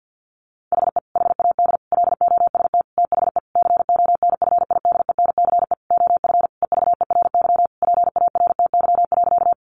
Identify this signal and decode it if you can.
Morse code, heard as a keyed tone, about 50 wpm, 700 Hz. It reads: HE HAD LOST THE CONFIDENCE OF EVERY PARTY.
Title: Sentences from Top 1500 Words 50wpm